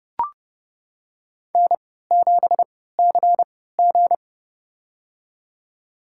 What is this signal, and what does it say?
Morse code, heard as a keyed tone, about 31 wpm, spaced as for 17 wpm, 700 Hz. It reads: N7CG
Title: USA Call Signs - 4 Characters - 3x Character Spacing - 30wpm